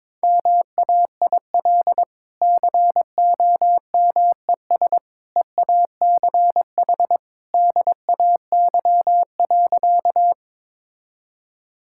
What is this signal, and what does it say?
Morse code, heard as a keyed tone, about 22 wpm, 700 Hz. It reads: MAIL COMES EACH DAY.
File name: practice / 22wpm / Easy Sentences with Hints - 22wpm